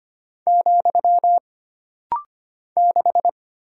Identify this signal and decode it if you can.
Morse code, heard as a keyed tone, about 25 wpm, 700 Hz. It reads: , 6
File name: Single Character (Letter:Number:Punctuation:Prosign) - Rapid-Fire - 25wpm